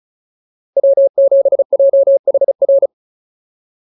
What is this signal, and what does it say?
Morse code, heard as a keyed tone, about 34 wpm, 560 Hz. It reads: W7JHR
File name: POTA307 35